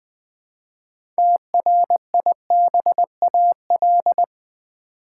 Morse code, heard as a keyed tone, about 20 wpm, 700 Hz. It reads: TRIBAL
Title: 6 Letter Words - First 3 Letters Given - 20wpm